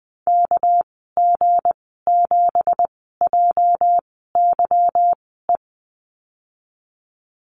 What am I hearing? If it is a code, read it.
Morse code, heard as a keyed tone, about 19 wpm, spaced as for 14 wpm, 700 Hz. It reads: KG7JYE
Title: 231 Lesson 37g - Letter x - Call Signs 20wpm